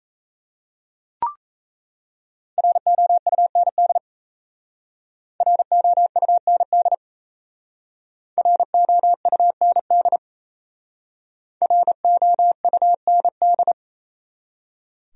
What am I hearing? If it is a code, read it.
Morse code, heard as a keyed tone, about 34 wpm, 700 Hz. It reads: ROUND ROUND ROUND ROUND
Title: Speed-Racing Top 200 Words 28wpm